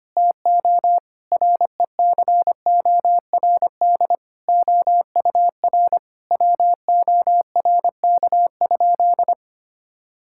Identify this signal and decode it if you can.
Morse code, heard as a keyed tone, about 25 wpm, 700 Hz. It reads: TO RECORD OUR WORK?
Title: Sentences from Top 1000 Words - Review Part 6 - 25wpm